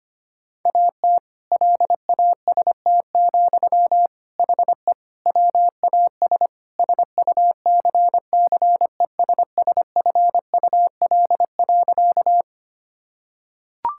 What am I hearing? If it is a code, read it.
Morse code, heard as a keyed tone, about 25 wpm, 700 Hz. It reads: AT LAST, HE WAS SUCCESSFUL.
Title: Sentences from Top 1000 Words - Review Part 6 - 25wpm